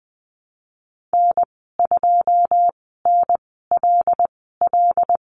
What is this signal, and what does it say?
Morse code, heard as a keyed tone, about 20 wpm, spaced as for 15 wpm, 700 Hz. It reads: N2NLL
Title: 082 Lesson 14g - Number 2 - Call Signs 20wpm